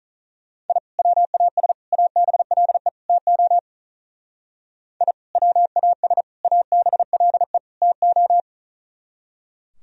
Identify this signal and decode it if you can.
Morse code, heard as a keyed tone, about 42 wpm, 700 Hz. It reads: I WAS ABLE TO I WAS ABLE TO
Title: Speed-Racing - Sets of 4 Words 35wpm